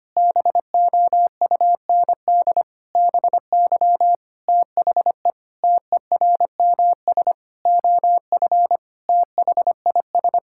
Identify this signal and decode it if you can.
Morse code, heard as a keyed tone, about 25 wpm, 700 Hz. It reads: BOUND BY THE TERMS OF THIS
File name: practice / 25wpm / Sentences with 9-Word Phrases - 25wpm